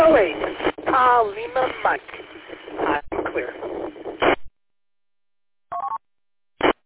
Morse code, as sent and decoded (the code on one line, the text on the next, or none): none